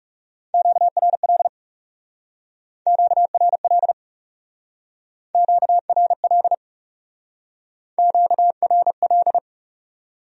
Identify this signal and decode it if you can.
Morse code, heard as a keyed tone, about 38 wpm, 700 Hz. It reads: QRL QRL QRL QRL